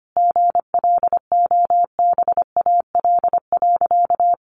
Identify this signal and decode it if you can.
Morse code, heard as a keyed tone, about 25 wpm, 700 Hz. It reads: GLOBAL.
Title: Sentences from Top 1200 Words 25wpm